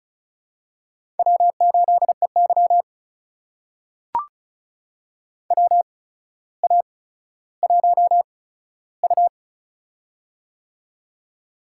Morse code, heard as a keyed tone, about 36 wpm, spaced as for 15 wpm, 700 Hz. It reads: W8EY WA1U
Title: USA Call Signs - 4 Characters - 8x Character Spacing - 35wpm